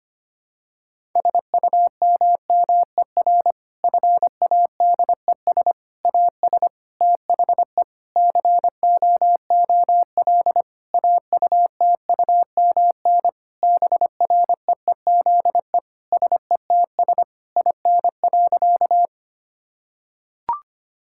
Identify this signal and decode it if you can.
Morse code, heard as a keyed tone, about 25 wpm, 700 Hz. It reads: SUMMER FADES AS THE COOL AUTUMN BREEZE SETS IN.
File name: practice / 25wpm / Difficult Sentences with Hints - 25wpm